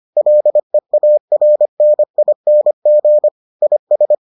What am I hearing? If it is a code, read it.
Morse code, heard as a keyed tone, about 25 wpm, 600 Hz. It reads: LEARNING IS